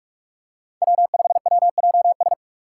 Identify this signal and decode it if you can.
Morse code, heard as a keyed tone, about 44 wpm, 700 Hz. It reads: W5WJS